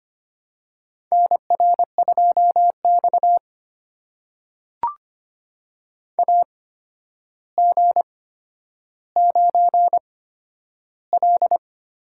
Morse code, heard as a keyed tone, about 25 wpm, spaced as for 11 wpm, 700 Hz. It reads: NR2X AG9L
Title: USA Call Signs - 4 Characters - 8x Character Spacing - 25wpm